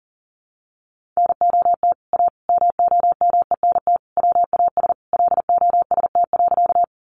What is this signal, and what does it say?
Morse code, heard as a keyed tone, about 40 wpm, 700 Hz. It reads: NOT A MOMENT WAS LOST.
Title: Sentences from Top 1400 Words 40wpm